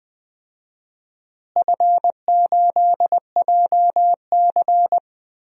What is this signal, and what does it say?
Morse code, heard as a keyed tone, about 20 wpm, 700 Hz. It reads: F8JC
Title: DXCC-100 Call Signs - 4 Characters - Repeated 2x - 20wpm